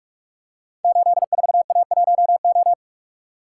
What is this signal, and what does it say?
Morse code, heard as a keyed tone, about 44 wpm, 700 Hz. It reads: 84A1O
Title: Code Groups - 5 Letter:Numbers - 45wpm